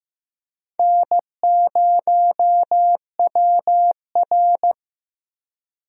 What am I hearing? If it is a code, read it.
Morse code, heard as a keyed tone, about 15 wpm, 700 Hz. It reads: N0WR
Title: USA Call Signs - 4 Characters - Repeated 2x - 15wpm